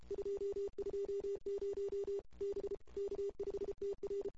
Morse code, heard as a keyed tone, about 31 wpm, 440 Hz. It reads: 220 B K5TR
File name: CW / 2008-ss-cw-k5tr-op-n5rz -23:00